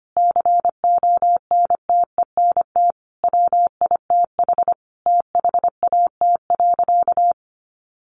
Morse code, heard as a keyed tone, about 25 wpm, 700 Hz. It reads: CONTENT WITH THAT.
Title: Sentences from Top 900 Words 25wpm